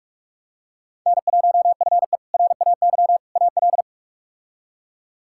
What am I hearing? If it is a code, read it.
Morse code, heard as a keyed tone, about 46 wpm, 700 Hz. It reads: N1FE RAY AL
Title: K1USN SST QSO Practice - 45wpm